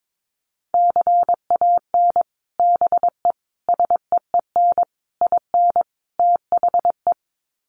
Morse code, heard as a keyed tone, about 22 wpm, 700 Hz. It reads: CAN BE SEEN IN THE
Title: Sets of 5 Words Encore2 22wpm